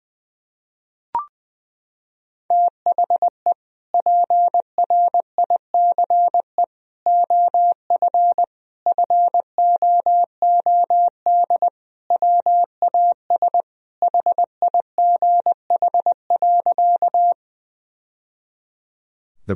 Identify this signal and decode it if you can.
Morse code, heard as a keyed tone, about 21 wpm, 700 Hz. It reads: THE PRICE OF FOOD WAS HIGH.
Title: Sentences from Top 1000 Words - Review Part 4 - 20wpm